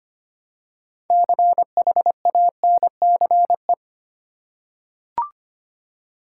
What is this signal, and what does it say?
Morse code, heard as a keyed tone, about 25 wpm, 700 Hz. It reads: CHANCE
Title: Six Letter Words - 25wpm